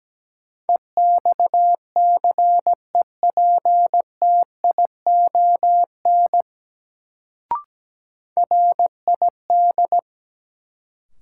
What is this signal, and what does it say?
Morse code, heard as a keyed tone, about 17 wpm, 700 Hz. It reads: EXCEPTION RID
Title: Sets of 4 Words - New Word Drill - 17wpm